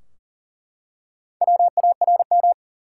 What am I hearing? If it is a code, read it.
Morse code, heard as a keyed tone, about 40 wpm, 700 Hz. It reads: WARM
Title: Four Letter Words - 40wpm